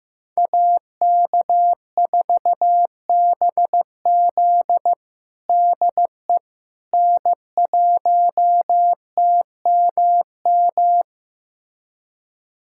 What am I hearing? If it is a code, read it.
Morse code, heard as a keyed tone, about 15 wpm, 700 Hz. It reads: AK4BZ DE N1TMM
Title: USA Callsign DE Callsign - 5 Characters - 15wpm